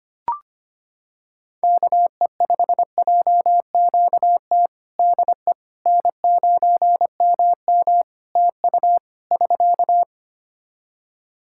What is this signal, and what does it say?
Morse code, heard as a keyed tone, about 25 wpm, 700 Hz. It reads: KE5JQT DE N9MM TU <SK>
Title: K1USN SST QSO Practice 25wpm